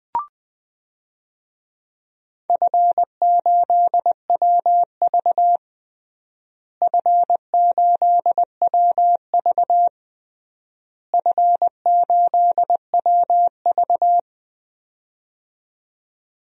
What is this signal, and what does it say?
Morse code, heard as a keyed tone, about 20 wpm, 700 Hz. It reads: F8WV F8WV F8WV
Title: DXCC-100 Call Signs - 4 Characters - Repeated 3x - 20wpm